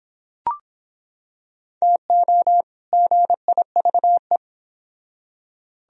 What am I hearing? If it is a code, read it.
Morse code, heard as a keyed tone, about 25 wpm, 700 Hz. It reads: TO GIVE